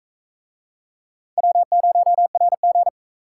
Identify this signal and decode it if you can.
Morse code, heard as a keyed tone, about 42 wpm, 700 Hz. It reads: W0RG